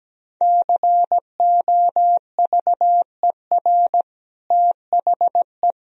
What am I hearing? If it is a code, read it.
Morse code, heard as a keyed tone, about 17 wpm, 700 Hz. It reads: COVER THE
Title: Sentences from Top 1000 Words - Review Part 1 - 17wpm